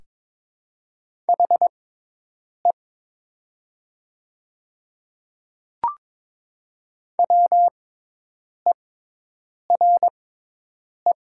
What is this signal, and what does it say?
Morse code, heard as a keyed tone, about 22 wpm, spaced as for 8 wpm, 700 Hz. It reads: HE WERE
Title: 6x Character Spacing - Top 200 Words - 22wpm